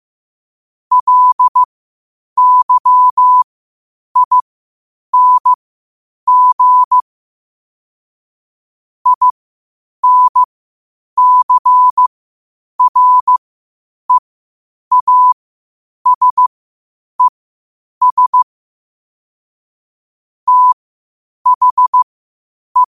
Morse code, heard as a keyed tone, about 15 wpm, spaced as for 8 wpm, 1000 Hz. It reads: LYING INCREASES THE
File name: qotdfiles / 8wpm